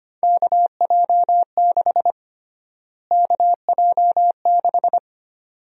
Morse code, heard as a keyed tone, about 25 wpm, 700 Hz. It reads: KJ6 KJ6
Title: USA Call Sign Prefixes - Repeated 2x - 25wpm